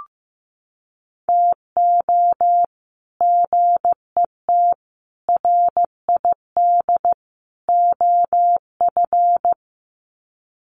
Morse code, heard as a keyed tone, about 15 wpm, 700 Hz. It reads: TO GET RID OF
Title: Sets of 4 Words 15wpm